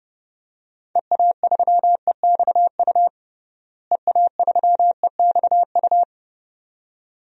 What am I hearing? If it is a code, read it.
Morse code, heard as a keyed tone, about 30 wpm, 700 Hz. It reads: EA3EXU EA3EXU
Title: DXCC-100 Call Signs - 6 Characters - Repeated 2x - 30wpm